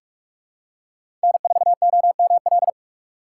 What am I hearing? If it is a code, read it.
Morse code, heard as a keyed tone, about 45 wpm, 690 Hz. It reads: N4OML
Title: USA Call Signs - 5 Characters - 4x Character Spacing - 45wpm